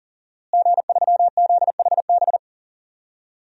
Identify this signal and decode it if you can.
Morse code, heard as a keyed tone, about 40 wpm, 700 Hz. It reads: G3ZHB
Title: DXCC-100 Call Signs - 5 Characters - Repeated 3x - 40wpm